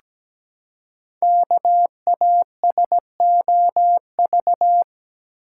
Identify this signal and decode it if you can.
Morse code, heard as a keyed tone, about 17 wpm, 700 Hz. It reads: KASOV
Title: Code Groups - 5 Letters - 17wpm